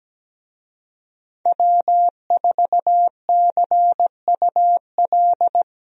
Morse code, heard as a keyed tone, about 17 wpm, 700 Hz. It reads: W4CUL
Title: USA Call Signs - 5 Characters - Repeated 2x - 17wpm